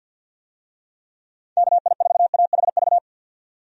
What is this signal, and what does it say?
Morse code, heard as a keyed tone, about 47 wpm, 700 Hz. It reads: KI4AHV
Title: USA Call Signs - 6 Characters - 50wpm